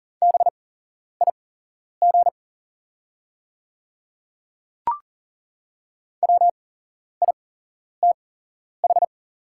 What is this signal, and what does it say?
Morse code, heard as a keyed tone, about 40 wpm, spaced as for 11 wpm, 690 Hz. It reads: BIG WITH
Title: 8x Character Spacing - Top 200 Words - 40wpm